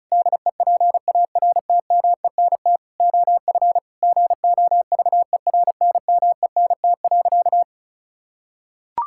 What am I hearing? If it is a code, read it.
Morse code, heard as a keyed tone, about 34 wpm, 700 Hz. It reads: DEPARTMENT OF GOVERNMENT.
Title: Sentences from Top 1000 Words - Review Part 5 - 35wpm